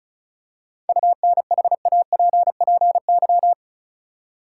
Morse code, heard as a keyed tone, about 36 wpm, 700 Hz. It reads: UNHAPPY
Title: Seven Letter Words - 35wpm